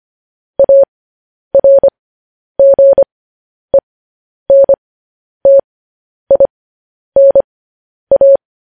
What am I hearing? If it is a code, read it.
Morse code, heard as a keyed tone, about 25 wpm, spaced as for 10 wpm, 550 Hz. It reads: ARGENTINA
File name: podcast / News headlines-1x-25x10